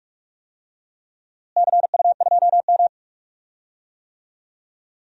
Morse code, heard as a keyed tone, about 44 wpm, 700 Hz. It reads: CU2M
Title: DXCC-100 Call Signs - 4 Characters - Repeated 2x - 45wpm